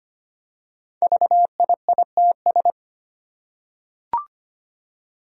Morse code, heard as a keyed tone, about 25 wpm, 700 Hz. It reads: VIITS
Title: Five Challenging Letter-Number Combinations 25wpm